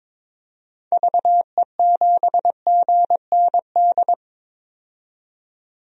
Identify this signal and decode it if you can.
Morse code, heard as a keyed tone, about 22 wpm, 700 Hz. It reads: VE7GND
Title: DXCC-100 Call Signs - 6 Characters - 22wpm